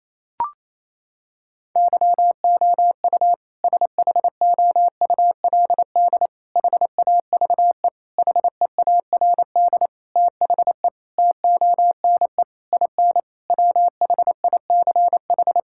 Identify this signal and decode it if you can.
Morse code, heard as a keyed tone, about 28 wpm, 700 Hz. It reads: YOU SHOULD HAVE HEARD THE TONE IN WHICH
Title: Sentences from Top 1600 Words 28wpm